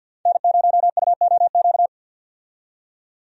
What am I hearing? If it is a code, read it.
Morse code, heard as a keyed tone, about 49 wpm, 700 Hz. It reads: N0UOQ
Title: USA Call Signs - 5 Characters - 50wpm